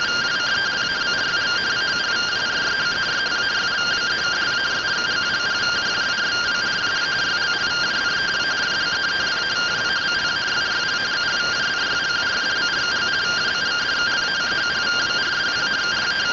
Two-tone teleprinter signal, about 170 Hz shift.
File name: findings / MR CORONAL fr1 -131.gen-09 rtty rx